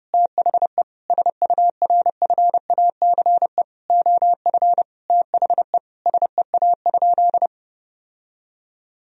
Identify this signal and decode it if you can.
Morse code, heard as a keyed tone, about 30 wpm, 700 Hz. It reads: THE SURFACE OF THE SEA?
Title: Sentences from Top 1000 Words - Review Part 3 - 30wpm